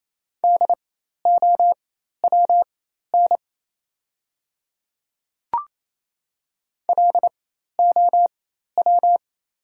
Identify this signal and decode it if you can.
Morse code, heard as a keyed tone, about 28 wpm, spaced as for 13 wpm, 700 Hz. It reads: DOWN LOW
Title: 4x Character Spacing - Top 200 Words - 28wpm